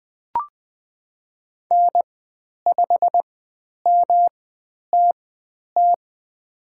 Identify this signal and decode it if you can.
Morse code, heard as a keyed tone, about 20 wpm, spaced as for 10 wpm, 700 Hz. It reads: N5MTT
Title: 109 Lesson 18g - Letter m - Call Signs 20wpm